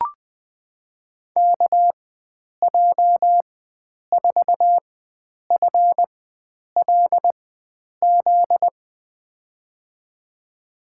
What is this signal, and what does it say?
Morse code, heard as a keyed tone, about 20 wpm, spaced as for 9 wpm, 700 Hz. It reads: KJ4FLZ
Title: USA Call Signs - 4x Character Spacing - 20wpm